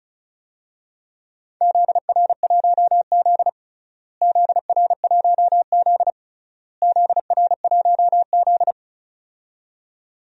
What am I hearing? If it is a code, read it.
Morse code, heard as a keyed tone, about 34 wpm, 700 Hz. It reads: ZR1Z ZR1Z ZR1Z
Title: DXCC-100 Call Signs - 4 Characters - Repeated 3x - 35wpm